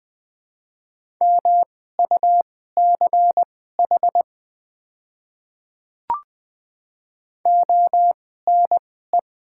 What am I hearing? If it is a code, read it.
Morse code, heard as a keyed tone, about 20 wpm, spaced as for 14 wpm, 700 Hz. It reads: MUCH ONE
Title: 2x Character Spacing - Top 200 Words - 20wpm